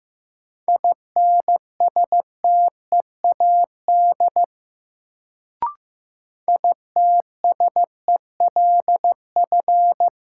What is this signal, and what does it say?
Morse code, heard as a keyed tone, about 15 wpm, 700 Hz. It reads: INSTEAD ITSELF
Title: Sentences from Top 800 Words - New Word Drill - 15wpm